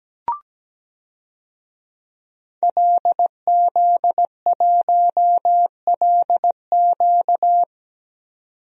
Morse code, heard as a keyed tone, about 17 wpm, 700 Hz. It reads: LZ1LQ
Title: DXCC-100 Call Signs - 5 Characters - 17wpm